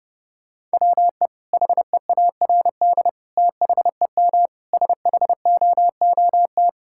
Morse code, heard as a keyed tone, about 28 wpm, 700 Hz. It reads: WE HEARD THEM SHOOT
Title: Sentences from Top 1600 Words 30wpm